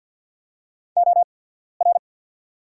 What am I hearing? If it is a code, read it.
Morse code, heard as a keyed tone, about 48 wpm, spaced as for 14 wpm, 690 Hz. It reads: OR
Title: 8x Character Spacing - Top 500 Words - 50wpm